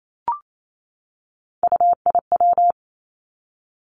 Morse code, heard as a keyed tone, about 28 wpm, 700 Hz. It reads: UIW
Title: Three Challenging Letter-Number Combinations 28wpm